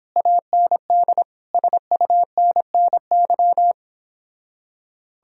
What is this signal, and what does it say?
Morse code, heard as a keyed tone, about 31 wpm, 700 Hz. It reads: AND SUNNY